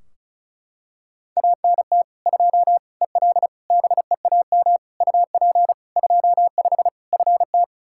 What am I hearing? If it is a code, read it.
Morse code, heard as a keyed tone, about 36 wpm, 700 Hz. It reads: ANT 2 EL BEAM UP 25 FT
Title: Practice QSO - Exchange 2 - Info - 35wpm